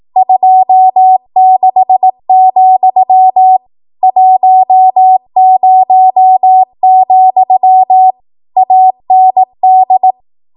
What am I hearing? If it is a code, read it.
Morse code, heard as a keyed tone, about 18 wpm, 750 Hz. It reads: 26, 10, AND
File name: Archive / 18 WPM / 18WPM